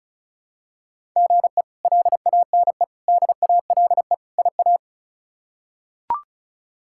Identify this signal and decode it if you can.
Morse code, heard as a keyed tone, about 36 wpm, 700 Hz. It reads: GE LANE DALE IA
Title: K1USN SST QSO Practice - 35wpm